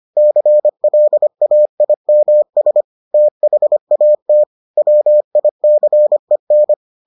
Morse code, heard as a keyed tone, about 25 wpm, 600 Hz. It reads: CLAIMS THAT WICEN